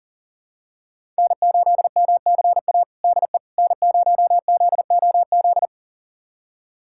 Morse code, heard as a keyed tone, about 40 wpm, 700 Hz. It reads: N8MCA DE N0ZOZ
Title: USA Callsign DE Callsign - 5 Characters - 40wpm